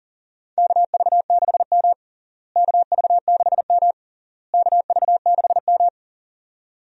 Morse code, heard as a keyed tone, about 40 wpm, 700 Hz. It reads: KV6M KV6M KV6M
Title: USA Call Signs - 4 Characters - Repeated 3x - 40wpm